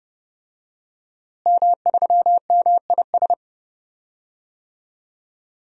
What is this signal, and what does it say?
Morse code, heard as a keyed tone, about 32 wpm, 700 Hz. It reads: M3MIS
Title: DXCC-100 Call Signs - 5 Characters - Repeated 2x - 30wpm